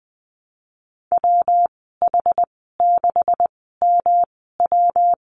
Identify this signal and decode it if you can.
Morse code, heard as a keyed tone, about 20 wpm, spaced as for 15 wpm, 700 Hz. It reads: WH6MW
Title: 154 Lesson 25g - Letter p - Call Signs 20wpm